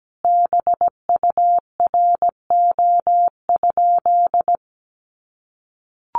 Morse code, heard as a keyed tone, about 17 wpm, 700 Hz. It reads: BURO?
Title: Common QSO Phrases 17wpm